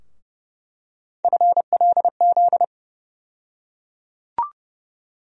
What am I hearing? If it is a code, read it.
Morse code, heard as a keyed tone, about 30 wpm, 700 Hz. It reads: FLZ
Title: USA Call Sign Postfixes - 3 Letters 30wpm